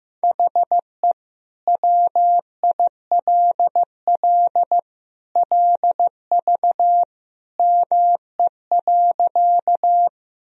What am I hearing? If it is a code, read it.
Morse code, heard as a keyed tone, about 15 wpm, 700 Hz. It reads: HE WILL LV ME.
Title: Sentences from Top 400 Words and Abbreviations 15wpm